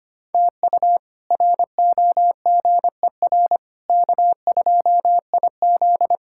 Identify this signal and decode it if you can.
Morse code, heard as a keyed tone, about 25 wpm, 700 Hz. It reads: TU ROGER K2IZ
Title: K1USN SST QSO Practice - 25wpm